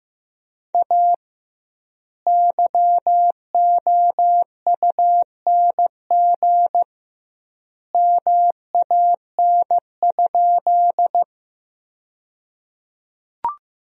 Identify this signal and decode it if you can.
Morse code, heard as a keyed tone, about 15 wpm, 700 Hz. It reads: A YOUNG MAN?
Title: 2x Word-Spacing - Sentences from Top 400 Words - 15wpm